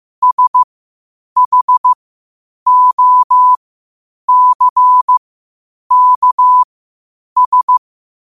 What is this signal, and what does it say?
Morse code, heard as a keyed tone, about 15 wpm, spaced as for 8 wpm, 1000 Hz. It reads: SHOCKS
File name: qotdfiles / 8wpm